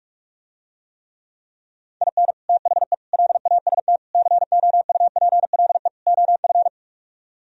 Morse code, heard as a keyed tone, about 44 wpm, 700 Hz. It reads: IN THE LAST COUPLE OF